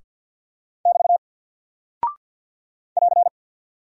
Morse code, heard as a keyed tone, about 48 wpm, 700 Hz. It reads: <BT> <AR>
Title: Single Character (Letter:Number:Punctuation:Prosign) Alt 50wpm